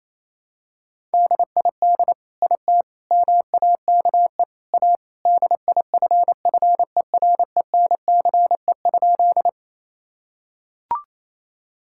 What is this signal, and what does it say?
Morse code, heard as a keyed tone, about 28 wpm, 700 Hz. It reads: DID IT MAKE A DIFFERENCE?
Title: Sentences from Top 1000 Words - Review Part 5 - 28wpm